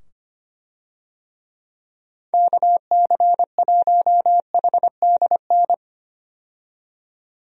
Morse code, heard as a keyed tone, about 25 wpm, 700 Hz. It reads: KC1HDN
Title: USA Call Signs - 6 Characters - 25wpm